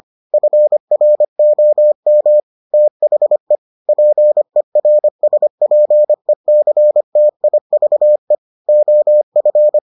Morse code, heard as a keyed tone, about 25 wpm, 600 Hz. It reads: FROM THE PERSPECTIVE OF